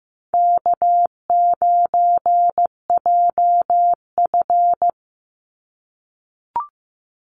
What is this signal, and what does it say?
Morse code, heard as a keyed tone, about 15 wpm, 700 Hz. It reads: K9JF
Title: Easy Call Signs - 15wpm